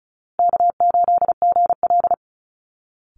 Morse code, heard as a keyed tone, about 34 wpm, 700 Hz. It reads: K8GL
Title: Easy Call Signs Repeated 3x - 35wpm